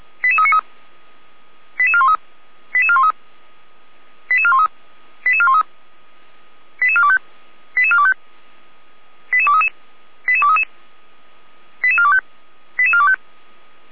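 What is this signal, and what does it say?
Five-tone selective call (ZVEI), heard as quick runs of five tones, tones of about 70 ms. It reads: 88393 88413 88413 88413 88413 88425 88425 88233 88233 88425 88425
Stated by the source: Five-tone ZVEI selective call received by SDR.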